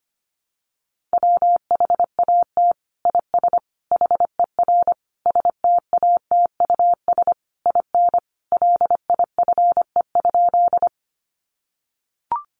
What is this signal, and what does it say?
Morse code, heard as a keyed tone, about 25 wpm, 700 Hz. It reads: WHAT IS HER STATUS IN LIFE?
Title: Sentences from Top 1400 Words 25wpm